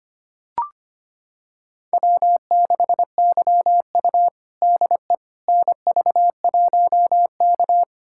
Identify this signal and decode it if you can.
Morse code, heard as a keyed tone, about 25 wpm, 700 Hz. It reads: W6YU DE NV1K
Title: USA Callsign DE Callsign - 4 Characters - 25wpm